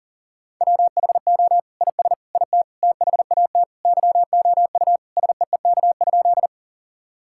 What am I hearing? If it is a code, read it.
Morse code, heard as a keyed tone, about 40 wpm, 700 Hz. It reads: WHO IS IT THAT YOU SEEK?
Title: Sentences from Top 1600 Words 40wpm